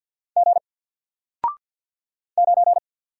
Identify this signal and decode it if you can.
Morse code, heard as a keyed tone, about 48 wpm, 700 Hz. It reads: G 9
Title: Single Letter-Number Alt 50wpm